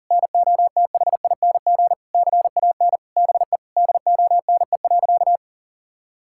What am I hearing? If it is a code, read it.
Morse code, heard as a keyed tone, about 40 wpm, 700 Hz. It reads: NOTHING CAN BE DONE.
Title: Sentences from Top 1600 Words 40wpm